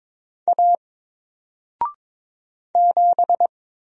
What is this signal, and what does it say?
Morse code, heard as a keyed tone, about 22 wpm, 700 Hz. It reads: A 7
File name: Single-Letter-Number-22wpm